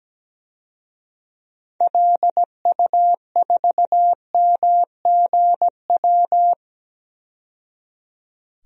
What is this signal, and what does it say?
Morse code, heard as a keyed tone, about 17 wpm, 700 Hz. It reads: LU4MGW
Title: DXCC-100 Call Signs - 6 Characters - 17wpm